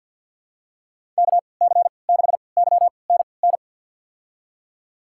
Morse code, heard as a keyed tone, about 52 wpm, spaced as for 28 wpm, 700 Hz. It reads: KC6YNN
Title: USA Call Signs - 3x Character Spacing - 50wpm